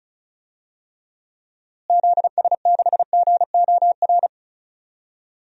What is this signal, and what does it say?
Morse code, heard as a keyed tone, about 36 wpm, 700 Hz. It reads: ZS6GOR
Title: DXCC-100 Call Signs - 6 Characters - Repeated 3x - 35wpm